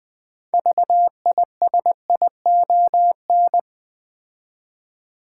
Morse code, heard as a keyed tone, about 20 wpm, 700 Hz. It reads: VISION
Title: 6 Letter Words - First 3 Letters Given - 20wpm